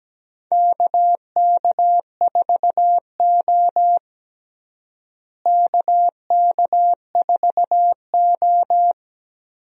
Morse code, heard as a keyed tone, about 17 wpm, 700 Hz. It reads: KK4O KK4O
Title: USA Call Signs - 4 Characters - Repeated 2x - 17wpm